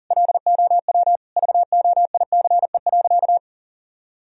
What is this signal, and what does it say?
Morse code, heard as a keyed tone, about 40 wpm, 700 Hz. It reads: LOW VOICE.
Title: Sentences from Top 1000 Words - Review Part 1 - 40wpm